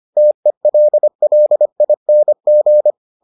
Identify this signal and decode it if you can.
Morse code, heard as a keyed tone, about 25 wpm, 600 Hz. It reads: TELLING